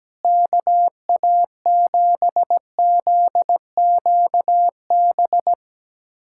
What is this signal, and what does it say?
Morse code, heard as a keyed tone, about 17 wpm, 700 Hz. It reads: KA7ZQB
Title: USA Callsign DE Callsign - 6 Characters - 17wpm